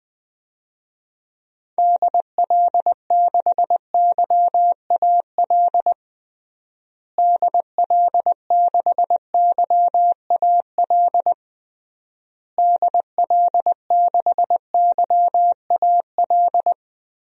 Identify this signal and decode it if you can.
Morse code, heard as a keyed tone, about 20 wpm, 700 Hz. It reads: DL6YAL DL6YAL DL6YAL
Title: DXCC-100 Call Signs - 6 Characters - Repeated 3x - 20wpm